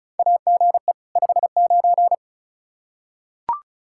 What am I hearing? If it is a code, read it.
Morse code, heard as a keyed tone, about 34 wpm, 700 Hz. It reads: AGE 59
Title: Practice QSO - Exchange 3 - More Info - 35wpm